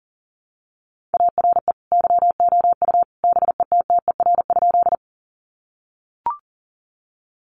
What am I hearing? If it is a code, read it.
Morse code, heard as a keyed tone, about 40 wpm, 700 Hz. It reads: ARE YOU BETTER?
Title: Sentences from Top 400 Words 40wpm